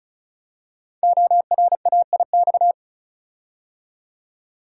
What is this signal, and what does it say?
Morse code, heard as a keyed tone, about 36 wpm, 700 Hz. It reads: ORAIX
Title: Code Groups - 5 Letters - 35wpm